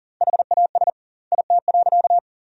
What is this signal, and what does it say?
Morse code, heard as a keyed tone, about 40 wpm, 700 Hz. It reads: HAS IT.
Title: 2x Word-Spacing - Sentences from Top 100 Words - 40wpm